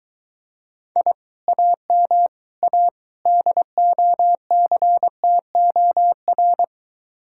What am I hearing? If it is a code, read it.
Morse code, heard as a keyed tone, about 22 wpm, 700 Hz. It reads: I AM A DOCTOR